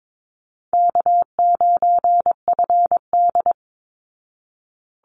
Morse code, heard as a keyed tone, about 22 wpm, 700 Hz. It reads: K9FD
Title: Call Signs Repeated 3x - 22wpm